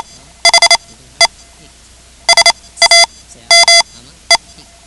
Morse code, heard as a keyed tone, about 28 wpm, spaced as for 15 wpm, 2050 Hz. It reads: HE SAME
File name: ham radio / CodeQuick / Code Quick 10